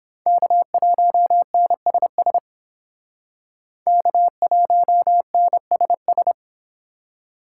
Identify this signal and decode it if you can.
Morse code, heard as a keyed tone, about 30 wpm, 700 Hz. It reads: K1NSS K1NSS